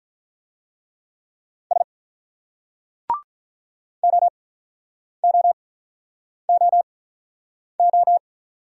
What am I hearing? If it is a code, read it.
Morse code, heard as a keyed tone, about 49 wpm, spaced as for 9 wpm, 700 Hz. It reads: S OOOO